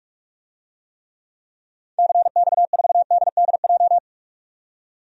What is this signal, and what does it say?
Morse code, heard as a keyed tone, about 42 wpm, 700 Hz. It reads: CX4DDJ
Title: DXCC-100 Call Signs - 6 Characters - 45wpm